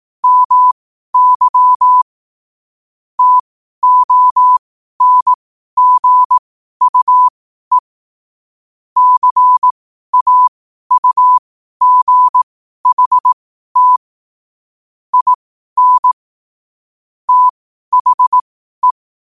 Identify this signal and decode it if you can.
Morse code, heard as a keyed tone, about 18 wpm, spaced as for 12 wpm, 1000 Hz. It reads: MY TONGUE CAUGHT IN THE
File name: qotdfiles / 12wpm